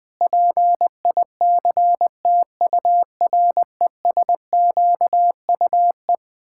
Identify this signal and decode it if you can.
Morse code, heard as a keyed tone, about 20 wpm, 700 Hz. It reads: PICTURESQUE
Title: Pangram Challenge v2 - 20wpm